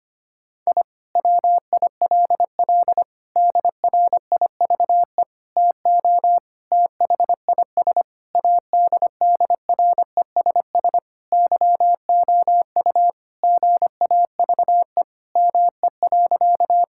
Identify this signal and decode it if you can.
Morse code, heard as a keyed tone, about 25 wpm, 700 Hz. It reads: I WILL DRIVE TO THIS ADDRESS YOU GAVE ME.
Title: Sentences from Top 1000 Words - Review Part 6 - 25wpm